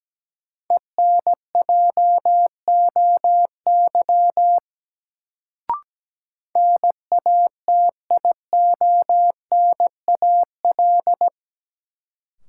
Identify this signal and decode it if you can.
Morse code, heard as a keyed tone, about 17 wpm, 700 Hz. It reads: ENJOY NATIONAL
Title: Sentences from Top 1000 Words - Review Part 4 - New Word Drill - 17wpm